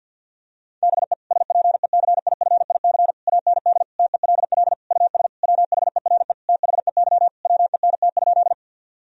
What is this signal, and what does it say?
Morse code, heard as a keyed tone, about 50 wpm, 700 Hz. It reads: BE SPECIFIC AND TELL US WHERE THEY WENT?
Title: Sentences from Top 1000 Words - Review Part 3 - 50wpm